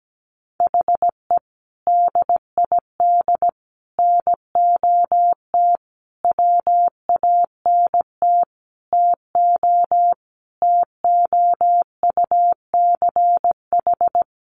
Morse code, heard as a keyed tone, about 17 wpm, 700 Hz. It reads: HE DID NOT WANT TO TOUCH
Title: Sentences from Top 1200 Words 17wpm